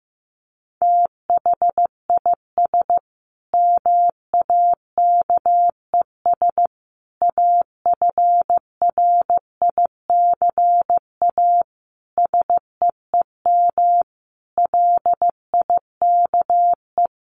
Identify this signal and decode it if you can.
Morse code, heard as a keyed tone, about 15 wpm, 700 Hz. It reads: THIS MAKES AFRICA SEEM LIKE
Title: A Sound Of Thunder 15wpm